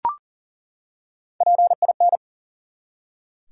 Morse code, heard as a keyed tone, about 40 wpm, 700 Hz. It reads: PIN
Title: Three Letter Words - 40wpm